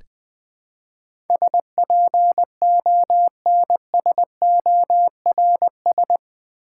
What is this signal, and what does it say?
Morse code, heard as a keyed tone, about 20 wpm, 700 Hz. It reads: SPONSORS
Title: Build A Word - 4 Letters At A Time - 11-Letter Words - 20wpm